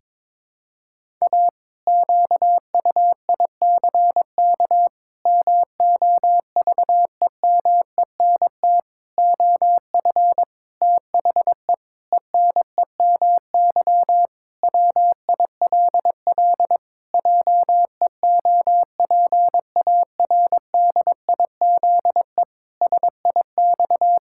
Morse code, heard as a keyed tone, about 22 wpm, 700 Hz. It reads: A QUICK MOVEMENT OF THE ENEMY WILL JEOPARDIZE SIX
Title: Pangram Challenge v2 - 22wpm